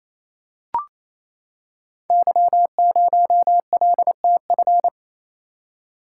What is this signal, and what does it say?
Morse code, heard as a keyed tone, about 28 wpm, 700 Hz. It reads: Y0LTF
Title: Code Groups - 5 Letter:Numbers - 28wpm